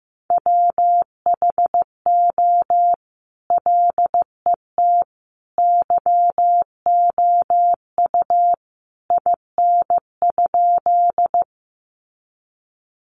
Morse code, heard as a keyed tone, about 15 wpm, 700 Hz. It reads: WHO LET YOU IN?
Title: Sentences from Top 300 Words 15wpm